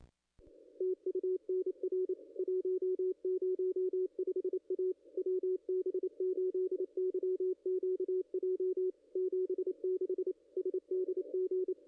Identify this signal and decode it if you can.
Morse code, heard as a keyed tone, about 28 wpm, 440 Hz, with radio noise. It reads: TU NR 105 A WB8YQJ 76 SDG
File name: contests / ss-cw-k5tr-op-n5rz -0500